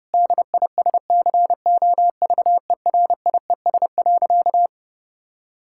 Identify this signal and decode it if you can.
Morse code, heard as a keyed tone, about 32 wpm, 700 Hz. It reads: DISCOVERIES.
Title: Pangram Challenge v2 - 30wpm